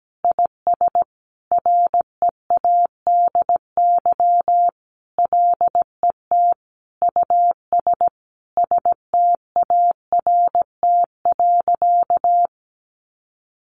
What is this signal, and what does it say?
Morse code, heard as a keyed tone, about 17 wpm, 700 Hz. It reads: IS READY LET US START.
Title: Sentences from Top 700 Words 17wpm